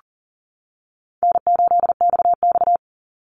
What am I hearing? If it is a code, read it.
Morse code, heard as a keyed tone, about 42 wpm, 700 Hz. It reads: N8XX
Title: Call Signs - 40wpm